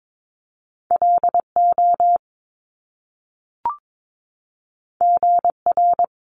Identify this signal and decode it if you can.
Morse code, heard as a keyed tone, about 22 wpm, 700 Hz. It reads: LO GR
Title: Top Two Letter Combinations in Words 22wpm